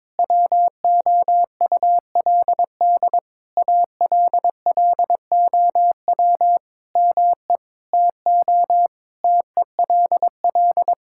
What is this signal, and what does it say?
Morse code, heard as a keyed tone, about 22 wpm, 700 Hz. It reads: WOULD ALLOW ME TO TELL
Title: Sentences from Top 1000 Words - Review Part 5 - 22wpm